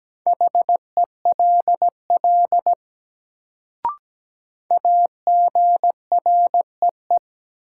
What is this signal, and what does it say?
Morse code, heard as a keyed tone, about 17 wpm, 700 Hz. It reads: HELL AGREE
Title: Sets of 5 Words Encore - New Word Drill - 17wpm